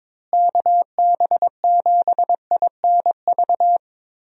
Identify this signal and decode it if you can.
Morse code, heard as a keyed tone, about 23 wpm, 700 Hz. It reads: KB7INV
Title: USA Call Signs - 6 Characters - Repeated 2x - 22wpm